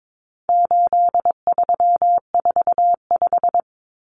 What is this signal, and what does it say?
Morse code, heard as a keyed tone, about 22 wpm, 700 Hz. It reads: 8345
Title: Common Cut Numbers - 4-Digit 22wpm